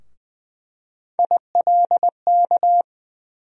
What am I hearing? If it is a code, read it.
Morse code, heard as a keyed tone, about 21 wpm, 700 Hz. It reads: ILK
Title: USA Call Sign Postfixes - 3 Letters 20wpm